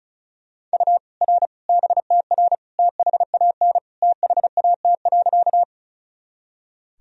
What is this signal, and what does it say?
Morse code, heard as a keyed tone, about 36 wpm, 700 Hz. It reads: U R BTR THAN THAT.
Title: Sentences from Top 400 Words and Abbreviations 35wpm